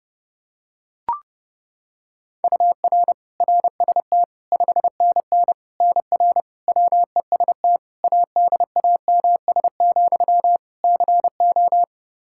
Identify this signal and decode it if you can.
Morse code, heard as a keyed tone, about 30 wpm, 700 Hz. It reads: UR RST 5NN NR WEST ADAMS, CO
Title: Practice QSO - Exchange 1 - Beginning - 30wpm